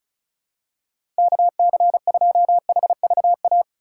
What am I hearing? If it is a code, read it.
Morse code, heard as a keyed tone, about 34 wpm, 700 Hz. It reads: KC2HVA
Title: USA Call Signs - 6 Characters - 2x Character Spacing - 35wpm